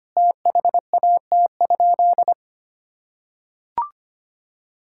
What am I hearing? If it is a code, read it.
Morse code, heard as a keyed tone, about 25 wpm, 700 Hz. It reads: THAT?
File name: practice / 25wpm / Sentences with 4-Word Phrases - 25wpm